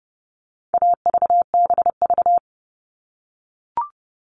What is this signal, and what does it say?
Morse code, heard as a keyed tone, about 30 wpm, 700 Hz. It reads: AVBV
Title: All Cut Numbers - 4-Digit 30wpm